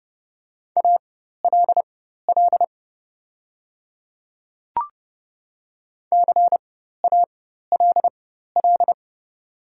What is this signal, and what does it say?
Morse code, heard as a keyed tone, about 30 wpm, spaced as for 14 wpm, 700 Hz. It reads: ALL CALL
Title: 4x Character Spacing - Top 100 Words - 30wpm